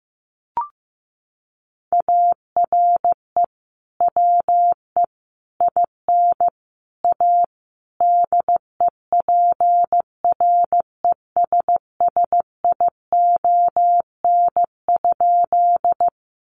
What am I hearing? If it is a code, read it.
Morse code, heard as a keyed tone, about 15 wpm, 700 Hz. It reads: ARE WE IN A DEPRESSION?
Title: Sentences from Top 1300 Words 15wpm